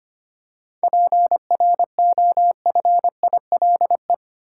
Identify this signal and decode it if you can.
Morse code, heard as a keyed tone, about 25 wpm, 700 Hz. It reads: PROFILE
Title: Seven Letter Words - 25wpm